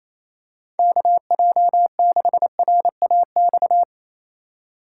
Morse code, heard as a keyed tone, about 28 wpm, 700 Hz. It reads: KJ6RAX
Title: USA Call Signs - 6 Characters - 6x Character Spacing - 28wpm